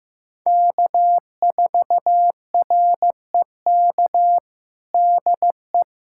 Morse code, heard as a keyed tone, about 15 wpm, 700 Hz. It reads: K4REK DE
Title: USA Callsign DE Callsign - 5 Characters - 15wpm